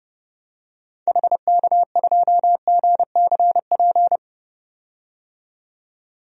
Morse code, heard as a keyed tone, about 31 wpm, 700 Hz. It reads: HK2GCP
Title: DXCC-100 Call Signs - 6 Characters - 30wpm